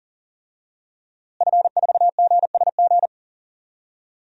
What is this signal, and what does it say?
Morse code, heard as a keyed tone, about 39 wpm, 700 Hz. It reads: F4GSG
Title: DXCC-100 Call Signs - 5 Characters - Repeated 3x - 40wpm